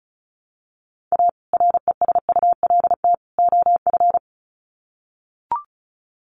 Morse code, heard as a keyed tone, about 34 wpm, 700 Hz. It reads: A RESULT OF
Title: Sets of 3 Words 35wpm